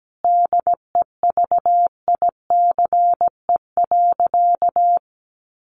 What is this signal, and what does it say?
Morse code, heard as a keyed tone, about 17 wpm, 700 Hz. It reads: DEVICE.
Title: Sentences from Top 1500 Words 17wpm